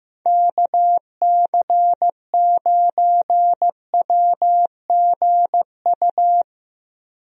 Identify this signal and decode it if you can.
Morse code, heard as a keyed tone, about 15 wpm, 700 Hz. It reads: KC9WGU
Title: USA Call Signs - 6 Characters - Repeated 3x - 15wpm